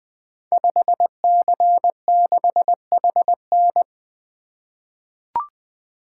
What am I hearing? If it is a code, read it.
Morse code, heard as a keyed tone, about 20 wpm, 700 Hz. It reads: 5C6HN
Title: Code Groups - 5 Letter:Numbers - 20wpm